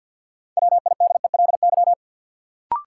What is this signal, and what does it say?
Morse code, heard as a keyed tone, about 50 wpm, 700 Hz. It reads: WIDELY
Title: Sentences from Top 1200 Words - New Word Drill Alt - 50wpm